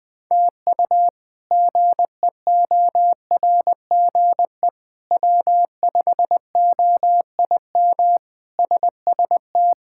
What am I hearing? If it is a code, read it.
Morse code, heard as a keyed tone, about 20 wpm, 700 Hz. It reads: TU GEORGE W5OIM SST
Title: K1USN SST QSO Practice - 20wpm